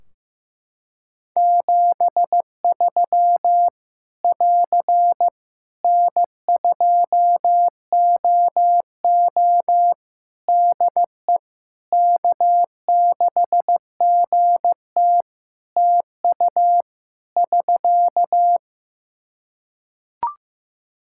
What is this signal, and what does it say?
Morse code, heard as a keyed tone, about 15 wpm, 700 Hz. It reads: 73 <AR> N2OO DE K6GT TU <SK>
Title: Practice QSO - Exchange 4 - Ending - 15wpm